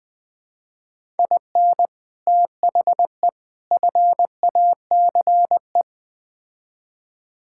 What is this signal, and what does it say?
Morse code, heard as a keyed tone, about 20 wpm, 690 Hz. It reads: IN THE FACE
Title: Build A Phrase - 4-Word Phrases - 20wpm